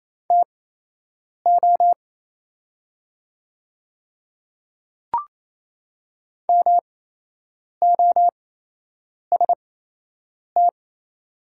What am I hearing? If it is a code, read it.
Morse code, heard as a keyed tone, about 28 wpm, spaced as for 8 wpm, 700 Hz. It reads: TO MOST
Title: 8x Character Spacing - Top 100 Words - 28wpm